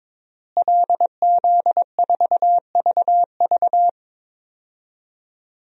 Morse code, heard as a keyed tone, about 22 wpm, 700 Hz. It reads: LZ4VV
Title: DXCC-100 Call Signs - 5 Characters - Repeated 2x - 22wpm